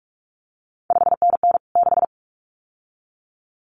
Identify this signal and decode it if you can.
Morse code, heard as a keyed tone, about 46 wpm, 700 Hz. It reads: 5NN 6
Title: CQ WW DX Contest Exchange - 45wpm